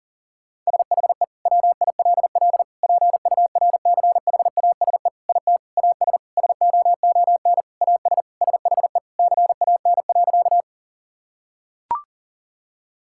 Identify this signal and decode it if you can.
Morse code, heard as a keyed tone, about 40 wpm, 700 Hz. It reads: SHE WILL PURCHASE IT AS SOON AS SHE CAN.
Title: Sentences from Top 1000 Words - Review Part 5 - 40wpm